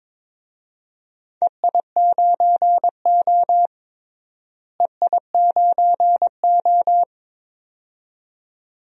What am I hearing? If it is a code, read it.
Morse code, heard as a keyed tone, about 22 wpm, 700 Hz. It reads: EI9O EI9O
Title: DXCC-100 Call Signs - 4 Characters - Repeated 2x - 22wpm